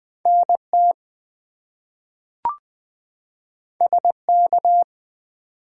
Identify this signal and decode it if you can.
Morse code, heard as a keyed tone, about 20 wpm, 700 Hz. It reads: NT SK
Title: US States and Canadian Provinces 20wpm